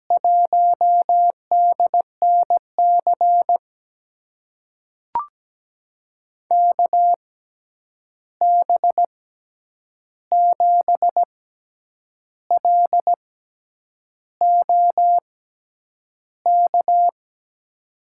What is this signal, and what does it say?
Morse code, heard as a keyed tone, about 17 wpm, spaced as for 8 wpm, 700 Hz. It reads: 1DNC KB7LOK
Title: USA Call Signs - 6 Characters - 6x Character Spacing - 17wpm